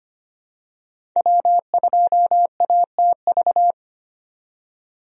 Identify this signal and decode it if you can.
Morse code, heard as a keyed tone, about 25 wpm, 700 Hz. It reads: W2ATV
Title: USA Call Signs - 5 Characters - 25wpm